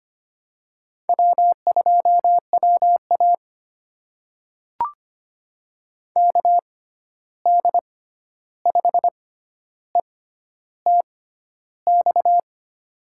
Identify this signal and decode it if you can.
Morse code, heard as a keyed tone, about 25 wpm, spaced as for 11 wpm, 700 Hz. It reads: W2WA KD5ETX
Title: USA Call Signs - 6x Character Spacing - 25wpm